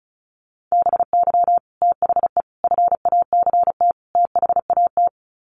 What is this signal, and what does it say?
Morse code, heard as a keyed tone, about 34 wpm, 700 Hz. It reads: BY THE FACT THAT
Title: Sets of 4 Words 35wpm